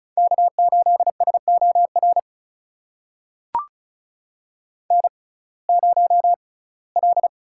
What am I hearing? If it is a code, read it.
Morse code, heard as a keyed tone, about 36 wpm, spaced as for 21 wpm, 700 Hz. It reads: K8SOR N0L
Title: USA Call Signs - 5 Characters - 6x Character Spacing - 35wpm